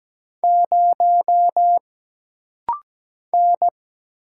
Morse code, heard as a keyed tone, about 17 wpm, 700 Hz. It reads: 0 N
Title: Single Letter-Number - Rapid-Fire 17wpm